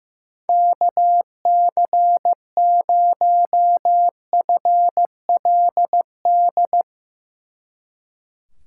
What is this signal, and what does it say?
Morse code, heard as a keyed tone, about 16 wpm, 700 Hz. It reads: KC0FLD
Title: Speed Racing - USA 2x3 Call Signs 15wpm